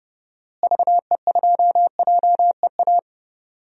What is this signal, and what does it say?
Morse code, heard as a keyed tone, about 30 wpm, 700 Hz. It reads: VE2JEA
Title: DXCC-100 Call Signs - 6 Characters - Repeated 2x - 30wpm